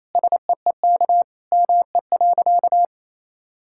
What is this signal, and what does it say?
Morse code, heard as a keyed tone, about 28 wpm, 700 Hz. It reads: SEEK ME.
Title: Sentences from Top 1600 Words 28wpm